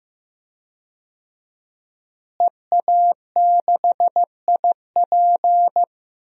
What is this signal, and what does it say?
Morse code, heard as a keyed tone, about 15 wpm, 700 Hz. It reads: EA6IP
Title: DXCC-100 Call Signs - 5 Characters - Repeated 3x - 15wpm